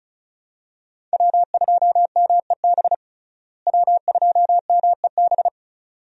Morse code, heard as a keyed tone, about 36 wpm, 700 Hz. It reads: W2MEB W2MEB
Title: USA Call Signs - 5 Characters - Repeated 2x - 35wpm